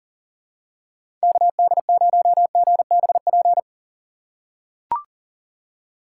Morse code, heard as a keyed tone, about 40 wpm, 700 Hz. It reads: KD0GBP
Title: USA Call Signs - 6 Characters - 3x Character Spacing - 40wpm